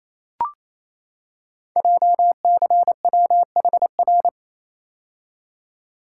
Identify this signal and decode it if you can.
Morse code, heard as a keyed tone, about 29 wpm, 700 Hz. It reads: JCWHR
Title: Code Groups - 5 Letters - 28wpm